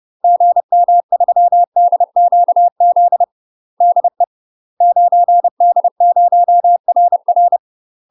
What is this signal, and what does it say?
Morse code, heard as a keyed tone, about 30 wpm, 700 Hz. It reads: GM3DQZ DE 9D0RR
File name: UK MASTER QSOs TRAINING / 30WPM / UK MASTER QSOs 0019